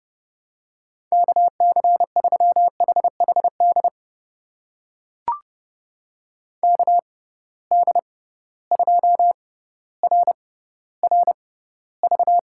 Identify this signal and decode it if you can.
Morse code, heard as a keyed tone, about 30 wpm, spaced as for 15 wpm, 700 Hz. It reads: KC3HHD KD2RRV
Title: USA Call Signs - 6 Characters - 6x Character Spacing - 30wpm